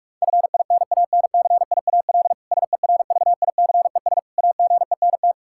Morse code, heard as a keyed tone, about 46 wpm, 700 Hz. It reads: FINANCIAL SERVICES AGENT
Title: K1USN SST QSO Practice 45wpm